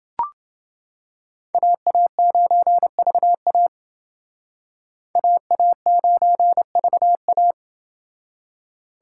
Morse code, heard as a keyed tone, about 28 wpm, 700 Hz. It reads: AA9VA AA9VA